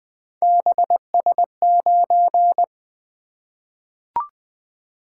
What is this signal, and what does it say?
Morse code, heard as a keyed tone, about 20 wpm, 700 Hz. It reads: BS9
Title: Three Letter-Number Combinations - 20wpm